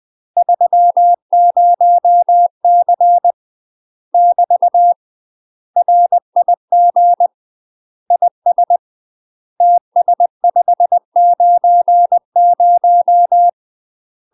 Morse code, heard as a keyed tone, about 20 wpm, 700 Hz. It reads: 30C <BT> RIG IS TS590
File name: UK MASTER QSOs TRAINING/20WPM